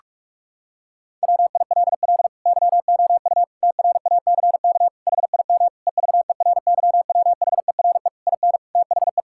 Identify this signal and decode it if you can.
Morse code, heard as a keyed tone, about 46 wpm, 700 Hz. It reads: WILL YOU TRACK HIM EVERYWHERE IN THE
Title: Sentences from Top 1600 Words 45wpm